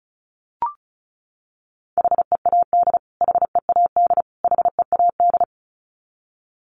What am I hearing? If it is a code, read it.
Morse code, heard as a keyed tone, about 34 wpm, 700 Hz. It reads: HEAD HEAD HEAD
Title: Top 500 Words Repeated 35wpm